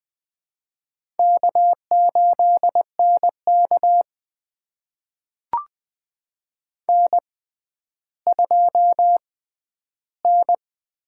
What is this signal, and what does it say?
Morse code, heard as a keyed tone, about 20 wpm, spaced as for 11 wpm, 700 Hz. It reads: K8NK N2N
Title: USA Call Signs - 4 Characters - 6x Character Spacing - 20wpm